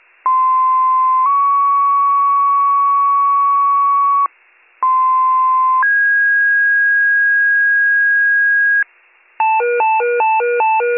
Two-tone page, heard as a one-second tone, then a three-second tone, about 1040 then 1120 Hz, about 1040 then 1650 Hz, with noise. It is heard from 0.3 to 8.8 s.